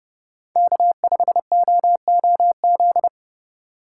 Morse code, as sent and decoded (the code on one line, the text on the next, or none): -.- ..... --- --- --..
K5OOZ